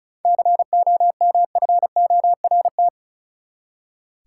Morse code, heard as a keyed tone, about 34 wpm, 700 Hz. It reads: COMFORT
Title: Seven Letter Words - 35wpm